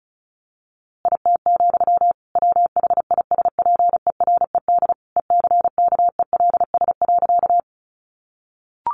Morse code, heard as a keyed tone, about 34 wpm, 700 Hz. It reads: IT, WHISPERED ECKELS.
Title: A Sound Of Thunder 35wpm